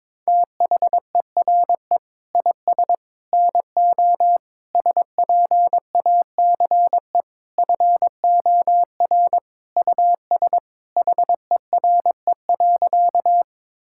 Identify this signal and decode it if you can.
Morse code, heard as a keyed tone, about 22 wpm, 700 Hz. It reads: THERE IS NO SPACE FOR US HERE.
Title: Sentences from Top 1000 Words - Review Part 5 - 22wpm